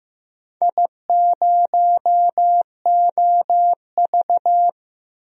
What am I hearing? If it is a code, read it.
Morse code, heard as a keyed tone, about 15 wpm, 700 Hz. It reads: I0OV
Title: DXCC-100 Call Signs - 4 Characters - 15wpm